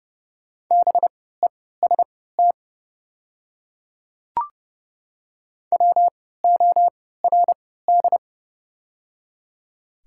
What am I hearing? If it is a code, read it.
Morse code, heard as a keyed tone, about 30 wpm, spaced as for 17 wpm, 700 Hz. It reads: BEST WORD
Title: 3x Character Spacing - Top 500 Words - 30wpm